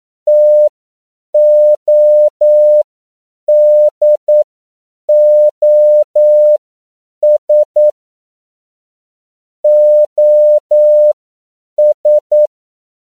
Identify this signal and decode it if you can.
Morse code, heard as a keyed tone, about 9 wpm, spaced as for 7 wpm, 600 Hz. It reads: TODOS OS